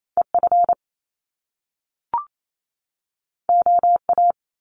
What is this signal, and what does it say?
Morse code, heard as a keyed tone, about 28 wpm, 700 Hz. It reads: E F O A
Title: Top Two Letter Combinations in Words 28wpm